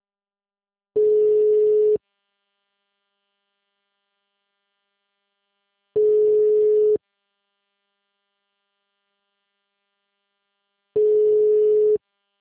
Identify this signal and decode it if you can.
Ringback tone (Europe, 425 Hz).
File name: out- -2005- - - .305